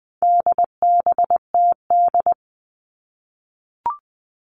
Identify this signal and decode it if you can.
Morse code, heard as a keyed tone, about 20 wpm, 700 Hz. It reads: DBTD
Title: All Cut Numbers - 4-Digit 20wpm